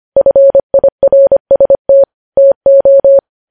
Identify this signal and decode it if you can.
Morse code, heard as a keyed tone, about 25 wpm, 550 Hz. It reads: FIRST TO
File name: podcast / News headlines-1x-25x25